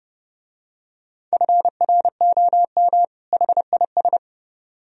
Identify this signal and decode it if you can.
Morse code, heard as a keyed tone, about 31 wpm, 700 Hz. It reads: FROM HIS